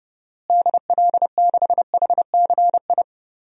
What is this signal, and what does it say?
Morse code, heard as a keyed tone, about 30 wpm, 700 Hz. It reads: DL6HCI
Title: DXCC-100 Call Signs - 6 Characters - Repeated 3x - 30wpm